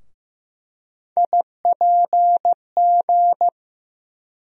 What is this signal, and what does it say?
Morse code, heard as a keyed tone, about 15 wpm, 700 Hz. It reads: IPG
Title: USA Call Sign Postfixes - 3 Letters 15wpm